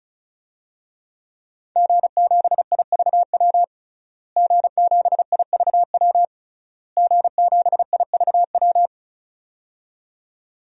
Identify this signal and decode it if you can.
Morse code, heard as a keyed tone, about 36 wpm, 700 Hz. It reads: G7IVW G7IVW G7IVW
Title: DXCC-100 Call Signs - 5 Characters - Repeated 3x - 35wpm